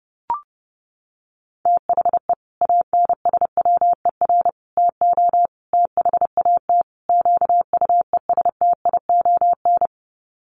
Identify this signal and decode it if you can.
Morse code, heard as a keyed tone, about 30 wpm, 700 Hz. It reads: THE ANSWER TO THAT QUESTION
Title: Sets of 5 Words Encore2 30wpm